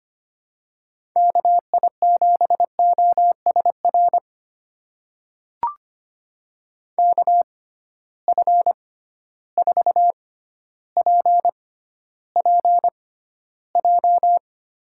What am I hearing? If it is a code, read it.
Morse code, heard as a keyed tone, about 25 wpm, spaced as for 13 wpm, 700 Hz. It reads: KI7OSR KF4PPJ
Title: USA Call Signs - 6 Characters - 6x Character Spacing - 25wpm